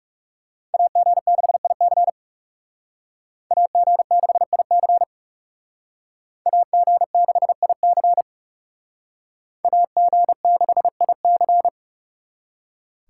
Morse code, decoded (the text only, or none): AG6IC AG6IC AG6IC AG6IC